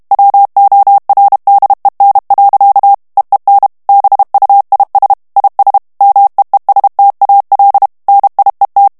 Morse code, heard as a keyed tone, about 32 wpm, 800 Hz. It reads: WORDEN. EEN BUIS IS MEESTAL NIET
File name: HOLLANDAIS texte clair 2 vitesse 30 mots minute